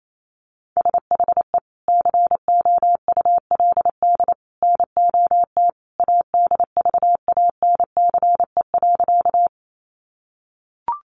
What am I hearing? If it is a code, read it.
Morse code, heard as a keyed tone, about 28 wpm, 700 Hz. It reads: SHE COULD NOT ADVANCE.
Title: Sentences from Top 1400 Words 28wpm